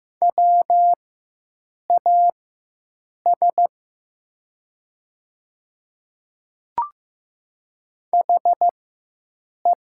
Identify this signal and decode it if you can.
Morse code, heard as a keyed tone, about 15 wpm, spaced as for 7 wpm, 700 Hz. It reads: WAS HE
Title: 4x Character Spacing - Top 100 Words - 15wpm